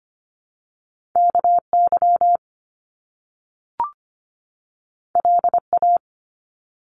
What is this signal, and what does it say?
Morse code, heard as a keyed tone, about 25 wpm, 700 Hz. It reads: KY LA
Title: US States 25wpm